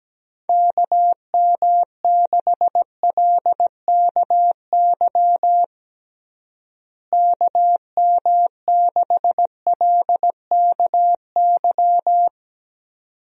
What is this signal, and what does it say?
Morse code, heard as a keyed tone, about 17 wpm, 700 Hz. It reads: KM6LKY KM6LKY
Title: USA Call Signs - 6 Characters - Repeated 2x - 17wpm